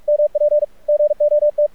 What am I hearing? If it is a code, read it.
Morse code, heard as a keyed tone, about 45 wpm, 590 Hz, with radio noise. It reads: MP GOT
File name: archived / CRSnet